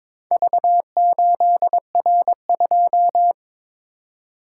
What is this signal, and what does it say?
Morse code, heard as a keyed tone, about 22 wpm, 700 Hz. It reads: V8R2
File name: practice / 22wpm / Four Letter-Number Combinations - 22wpm